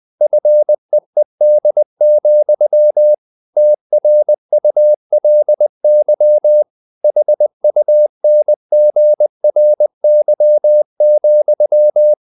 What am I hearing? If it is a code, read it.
Morse code, heard as a keyed tone, about 20 wpm, 600 Hz. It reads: FEED, TRULY HUNGRY,
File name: the old man and the sea part 2 20wpm